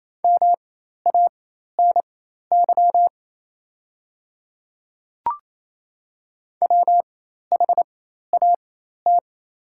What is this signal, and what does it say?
Morse code, heard as a keyed tone, about 27 wpm, spaced as for 13 wpm, 700 Hz. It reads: MANY WHAT
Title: 4x Character Spacing - Top 100 Words - 28wpm